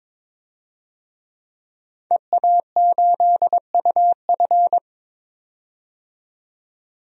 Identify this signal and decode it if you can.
Morse code, heard as a keyed tone, about 22 wpm, 700 Hz. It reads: EA8UF
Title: DXCC-100 Call Signs - 5 Characters - 22wpm